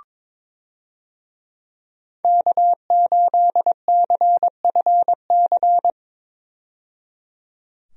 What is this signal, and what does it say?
Morse code, heard as a keyed tone, about 22 wpm, 700 Hz. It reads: K8CFC
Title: USA Call Signs - 5 Characters - 22wpm